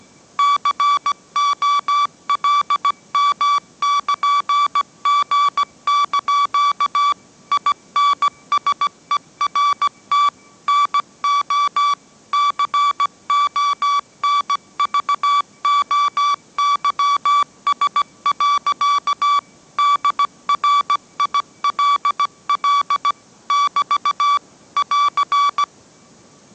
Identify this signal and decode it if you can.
Morse code, heard as a keyed tone, about 17 wpm, 1150 Hz. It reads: COLM<KN>G) INSERT NO CONVOYS. DRILL <BT> <AR>
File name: mrx / mrx080 18wpm